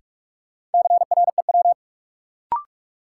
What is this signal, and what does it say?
Morse code, heard as a keyed tone, about 43 wpm, 700 Hz. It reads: CREW
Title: Sentences from Top 1500 Words - New Word Drill - 45wpm